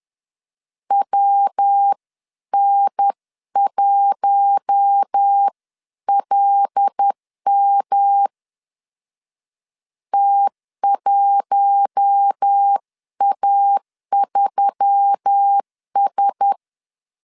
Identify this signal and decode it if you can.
Morse code, heard as a keyed tone, about 10 wpm, 800 Hz. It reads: WN1LM T1A3S
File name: CW / DET / 1 6 1500C S B 10wpm 4137